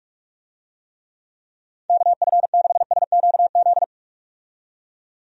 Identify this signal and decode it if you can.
Morse code, heard as a keyed tone, about 46 wpm, 700 Hz. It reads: KF6SQZ
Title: USA Call Signs - 6 Characters - 45wpm